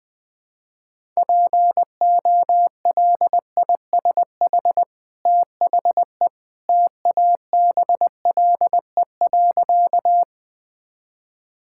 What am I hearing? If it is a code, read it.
Morse code, heard as a keyed tone, about 20 wpm, 700 Hz. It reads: POLISH THE TABLE.
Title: Easy Sentences with Hints - 20wpm